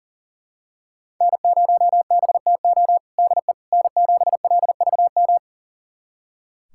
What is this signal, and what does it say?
Morse code, heard as a keyed tone, about 40 wpm, 700 Hz. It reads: N0BTO DE N7LVM
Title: USA Callsign DE Callsign - 5 Characters - 40wpm